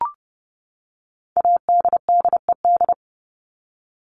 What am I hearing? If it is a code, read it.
Morse code, heard as a keyed tone, about 30 wpm, 700 Hz. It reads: ADDED
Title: ED Suffix - Top 100 Words with Base Words 30wpm